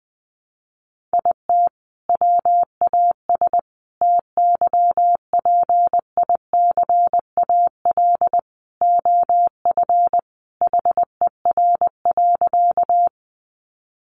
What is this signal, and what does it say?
Morse code, heard as a keyed tone, about 20 wpm, 700 Hz. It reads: IT WAS TYPICAL OF HER.
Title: Sentences from Top 1100 Words 20wpm